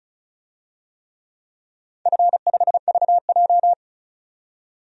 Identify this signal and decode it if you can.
Morse code, heard as a keyed tone, about 36 wpm, 700 Hz. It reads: F5VJ
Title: DXCC-100 Call Signs - 4 Characters - 35wpm